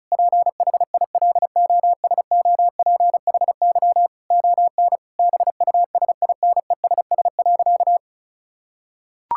Morse code, heard as a keyed tone, about 34 wpm, 700 Hz. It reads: PHILOSOPHY ON BUSINESS.
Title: Sentences from Top 1000 Words - Review Part 6 - 35wpm